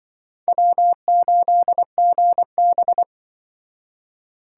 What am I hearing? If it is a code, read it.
Morse code, heard as a keyed tone, about 24 wpm, 700 Hz. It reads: W8GB
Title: Speed Racing - USA 1x2 Call Signs 20wpm